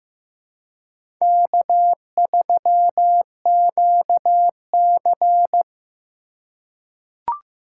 Morse code, heard as a keyed tone, about 15 wpm, 700 Hz. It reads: K3QC
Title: USA Call Signs - 4 Characters - 2x Character Spacing - 15wpm